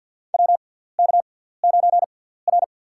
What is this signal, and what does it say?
Morse code, heard as a keyed tone, about 53 wpm, spaced as for 17 wpm, 690 Hz. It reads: WK9R
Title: USA Call Signs - 6x Character Spacing - 50wpm